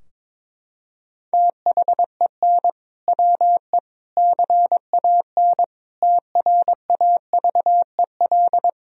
Morse code, heard as a keyed tone, about 22 wpm, 700 Hz. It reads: THEN WE CAN TRAVEL
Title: Sentences from Top 1000 Words - Review Part 2 - 22wpm